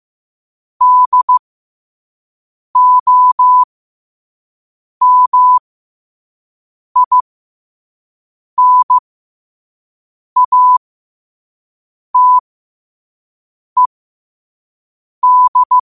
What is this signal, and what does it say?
Morse code, heard as a keyed tone, about 15 wpm, spaced as for 5 wpm, 1000 Hz. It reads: DOMINATED
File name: qotdfiles / 5wpm